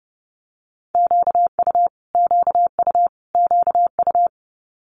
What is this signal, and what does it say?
Morse code, heard as a keyed tone, about 31 wpm, 700 Hz. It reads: QU QU QU
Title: Multi-letter Phonemes Repeated 3x 30wpm